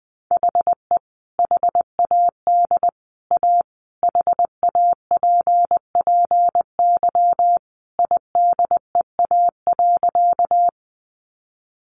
Morse code, heard as a keyed tone, about 19 wpm, 700 Hz. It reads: HE HAD A HAPPY IDEA.
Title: Sentences from Top 1000 Words 20wpm